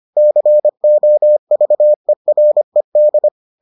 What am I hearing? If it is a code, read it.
Morse code, heard as a keyed tone, about 25 wpm, 600 Hz. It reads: COVERED